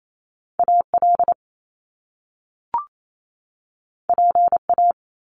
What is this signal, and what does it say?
Morse code, heard as a keyed tone, about 28 wpm, 700 Hz. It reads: AL PA
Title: Top Two Letter Combinations in Words 28wpm